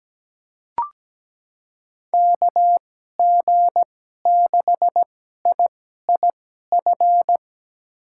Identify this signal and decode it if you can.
Morse code, heard as a keyed tone, about 17 wpm, spaced as for 12 wpm, 700 Hz. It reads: KG6IIF
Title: USA Call Signs - 2x Character Spacing - 17wpm